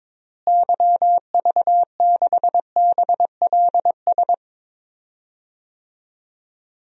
Morse code, heard as a keyed tone, about 22 wpm, 700 Hz. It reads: YV6BLS
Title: DXCC-100 Call Signs - 6 Characters - Repeated 2x - 22wpm